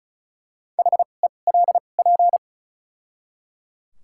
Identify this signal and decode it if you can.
Morse code, heard as a keyed tone, about 34 wpm, spaced as for 25 wpm, 700 Hz. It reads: HELP
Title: 2x Character Spacing - Top 300 Words - 35wpm